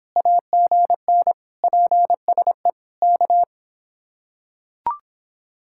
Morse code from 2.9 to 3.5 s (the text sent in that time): K